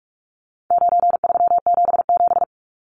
Morse code, heard as a keyed tone, about 44 wpm, 700 Hz. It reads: 9377
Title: Numbers - 4-Digit 45wpm